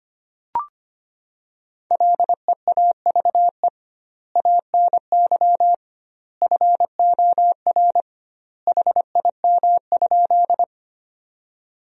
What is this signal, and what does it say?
Morse code, heard as a keyed tone, about 25 wpm, 700 Hz. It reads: LEAVE ANY FOR HIM?
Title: 2x Word-Spacing - Sentences from Top 400 Words - 25wpm